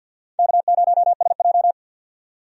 Morse code, heard as a keyed tone, about 52 wpm, 700 Hz. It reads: K0SJ
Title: USA Call Signs - 4 Characters - 50wpm